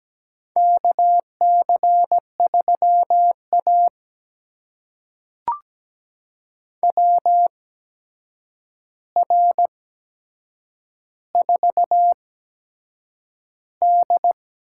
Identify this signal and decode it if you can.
Morse code, heard as a keyed tone, about 17 wpm, spaced as for 7 wpm, 700 Hz. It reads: KC3A WR4D
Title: USA Call Signs - 4 Characters - 8x Character Spacing - 17wpm